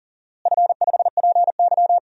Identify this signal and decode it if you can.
Morse code, heard as a keyed tone, about 40 wpm, 700 Hz. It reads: F5PY
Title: DXCC-100 Call Signs - 4 Characters - Repeated 3x - 40wpm